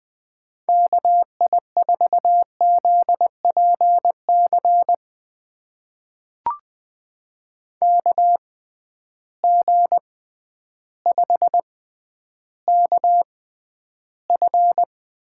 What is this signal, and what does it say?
Morse code, heard as a keyed tone, about 20 wpm, spaced as for 11 wpm, 700 Hz. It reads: KI4ZPC KG5KF
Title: USA Call Signs - 6 Characters - 6x Character Spacing - 20wpm